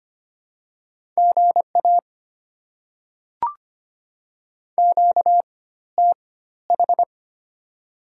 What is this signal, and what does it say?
Morse code, heard as a keyed tone, about 25 wpm, spaced as for 14 wpm, 700 Hz. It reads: GA QTH
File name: Top 20 QSO Elements - 4x Character Spacing - 25wpm